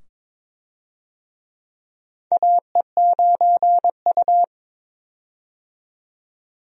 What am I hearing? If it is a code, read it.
Morse code, heard as a keyed tone, about 22 wpm, 700 Hz. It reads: AE9U
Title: USA Call Signs - 4 Characters - 22wpm